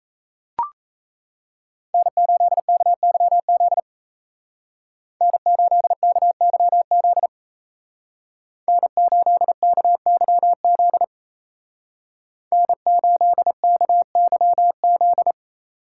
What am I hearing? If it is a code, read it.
Morse code, heard as a keyed tone, about 37 wpm, 700 Hz. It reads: N8KYZ N8KYZ N8KYZ N8KYZ